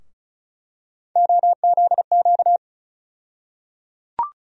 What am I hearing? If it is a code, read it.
Morse code, heard as a keyed tone, about 36 wpm, 700 Hz. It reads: OZQ
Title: USA Call Sign Postfixes - 3 Letters 35wpm